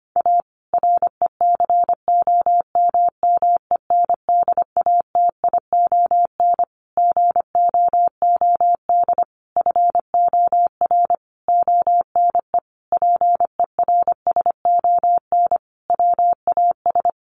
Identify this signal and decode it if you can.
Morse code, heard as a keyed tone, about 25 wpm, 700 Hz. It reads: A RECOMMENDATION GOOD FOR ONE PERSON WAS
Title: Sentences from Top 1400 Words 25wpm